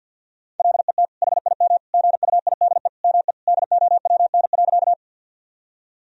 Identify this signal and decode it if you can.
Morse code, heard as a keyed tone, about 50 wpm, 700 Hz. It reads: LET HIM GUIDE ME DOWN.
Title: Sentences from Top 1000 Words - Review Part 3 - 50wpm